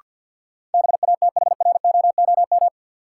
Morse code, heard as a keyed tone, about 46 wpm, 700 Hz. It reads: BATHROOM
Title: Sets of 5 Words Encore - New Word Drill - 50wpm